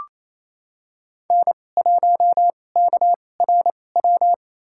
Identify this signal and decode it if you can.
Morse code, heard as a keyed tone, about 28 wpm, spaced as for 20 wpm, 700 Hz. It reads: N1KRW
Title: USA Call Signs - 5 Characters - 2x Character Spacing - 28wpm